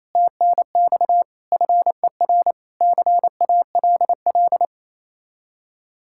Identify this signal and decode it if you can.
Morse code, heard as a keyed tone, about 33 wpm, 700 Hz. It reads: TNX FER CALL
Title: Speed-Racing Common QSO Phrases - 28wpm